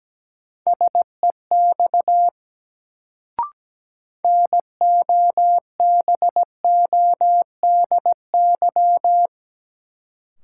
Morse code, heard as a keyed tone, about 17 wpm, 700 Hz. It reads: SEX NOBODY
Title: Sentences from Top 1600 Words - New Word Drill - 17wpm